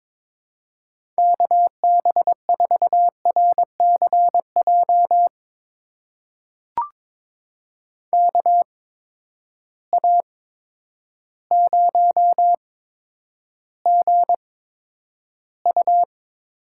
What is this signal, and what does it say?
Morse code, heard as a keyed tone, about 22 wpm, spaced as for 10 wpm, 700 Hz. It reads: KB4RCJ KA0GU
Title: USA Call Signs - 6 Characters - 8x Character Spacing - 22wpm